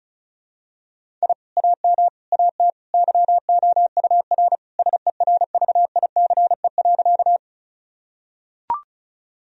Morse code, heard as a keyed tone, about 34 wpm, 700 Hz. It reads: I AM AT YOUR SERVICE.
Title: Sentences from Top 1000 Words - Review Part 4 - 35wpm